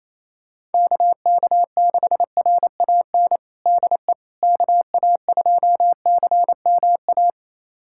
Morse code, heard as a keyed tone, about 28 wpm, 700 Hz. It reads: KK6RAN DE KA2CMA
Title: USA Callsign DE Callsign - 6 Characters - 28wpm